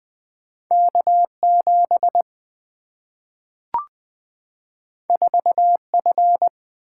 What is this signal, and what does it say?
Morse code, heard as a keyed tone, about 20 wpm, 700 Hz. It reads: K7 4F
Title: Single-Letter Single-Number 20wpm